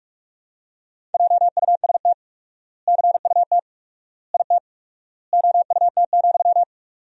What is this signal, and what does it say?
Morse code, heard as a keyed tone, about 46 wpm, 700 Hz. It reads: JUST CUT IT OUT,
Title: 4x Word-Spacing - Sentences from Top 300 Words - 45wpm